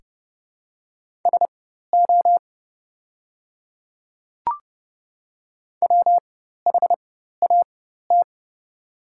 Morse code, heard as a keyed tone, about 30 wpm, spaced as for 14 wpm, 700 Hz. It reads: SO WHAT
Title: 4x Character Spacing - Top 300 Words - 30wpm